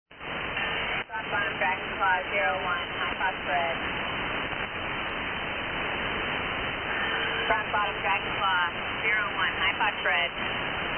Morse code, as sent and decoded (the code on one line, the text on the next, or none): none